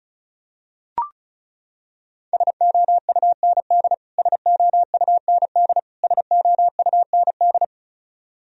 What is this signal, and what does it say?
Morse code, heard as a keyed tone, about 36 wpm, 700 Hz. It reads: SOUND SOUND SOUND
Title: Top 100 Words Repeated 35wpm